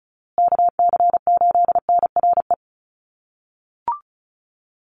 Morse code, heard as a keyed tone, about 34 wpm, 700 Hz. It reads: KC8NRE
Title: USA Call Signs Repeated 3x 35wpm